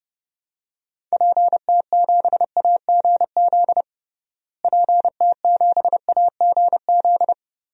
Morse code, heard as a keyed tone, about 32 wpm, 700 Hz. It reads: PT7AGZ PT7AGZ
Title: DXCC-100 Call Signs - 6 Characters - Repeated 3x - 30wpm